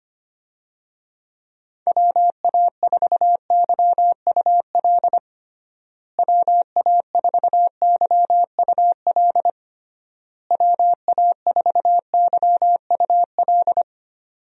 Morse code, heard as a keyed tone, about 25 wpm, 700 Hz. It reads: WA4YUL WA4YUL WA4YUL
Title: USA Call Signs - 6 Characters - Repeated 3x - 25wpm